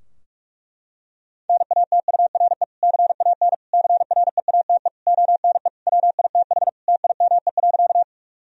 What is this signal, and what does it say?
Morse code, heard as a keyed tone, about 42 wpm, 700 Hz. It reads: NATURE CAN CREATE ONE WITH TIME.
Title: Sentences from Top 1000 Words - Review Part 3 - 45wpm